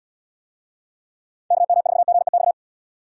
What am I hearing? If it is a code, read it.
Morse code, heard as a keyed tone, about 76 wpm, 700 Hz. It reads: KM4GEX